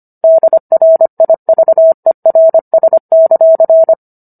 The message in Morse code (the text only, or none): DRIVERS;